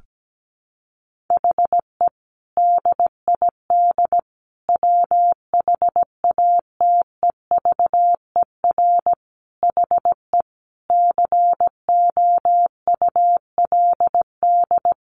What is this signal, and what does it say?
Morse code, heard as a keyed tone, about 17 wpm, 700 Hz. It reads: HE DID WHATEVER HE COULD
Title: Sentences from Top 1100 Words 17wpm